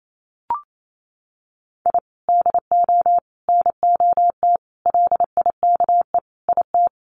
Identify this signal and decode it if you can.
Morse code, heard as a keyed tone, about 28 wpm, 700 Hz. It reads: I DO NOT LIKE IT
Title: Sets of 5 Words Encore 28wpm